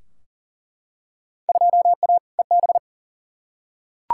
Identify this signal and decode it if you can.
Morse code, heard as a keyed tone, about 40 wpm, 700 Hz. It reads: 2A EB
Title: Field Day Exchanges - 40wpm